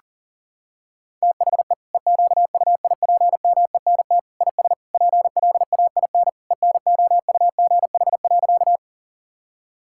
Morse code, heard as a keyed tone, about 37 wpm, 700 Hz. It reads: THE EQUIPMENT IS PLAIN ENOUGH.
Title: Sentences from Top 1000 Words - Review Part 5 - 40wpm